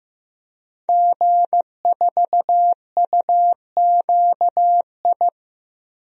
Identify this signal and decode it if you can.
Morse code, heard as a keyed tone, about 15 wpm, 700 Hz. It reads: G4UQI
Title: DXCC-100 Call Signs - 5 Characters - Repeated 2x - 15wpm